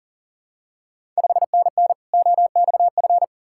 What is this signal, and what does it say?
Morse code, heard as a keyed tone, about 40 wpm, 690 Hz. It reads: 5NN OXF
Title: Ontario QSO Party Exchange 40wpm